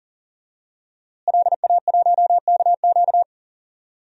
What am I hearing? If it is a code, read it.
Morse code, heard as a keyed tone, about 40 wpm, 700 Hz. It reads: LA1KQ
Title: DXCC-100 Call Signs - 5 Characters - Repeated 3x - 40wpm